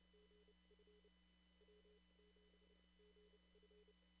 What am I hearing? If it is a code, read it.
Morse code, heard as a keyed tone, about 31 wpm, spaced as for 24 wpm, 420 Hz, with noise. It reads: GF W7GF
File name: CW / kt5j wpx cw-op-k5pi